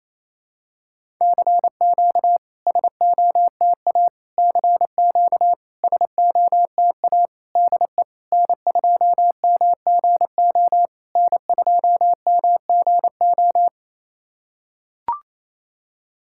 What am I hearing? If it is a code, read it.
Morse code, heard as a keyed tone, about 28 wpm, 700 Hz. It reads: CQ SOTA CQ SOTA DE N2MGO N2MGO
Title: SOTA QSO Practice 28wpm